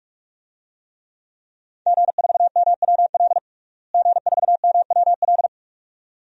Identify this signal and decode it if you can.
Morse code, heard as a keyed tone, about 41 wpm, 700 Hz. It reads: G4MWL G4MWL
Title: DXCC-100 Call Signs - 5 Characters - Repeated 2x - 45wpm